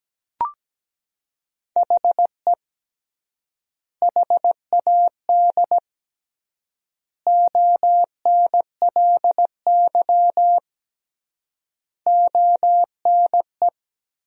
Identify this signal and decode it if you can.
Morse code, heard as a keyed tone, about 17 wpm, 700 Hz. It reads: HE HAD ONLY ONE
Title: 3x Word-Spacing - Sentences from Top 300 Words - 17wpm